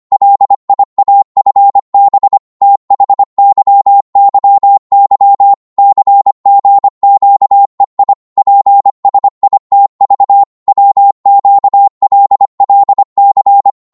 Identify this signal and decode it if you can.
Morse code, heard as a keyed tone, about 25 wpm, 800 Hz. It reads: LIAFB THYYY CGQEI PSITV WQLLC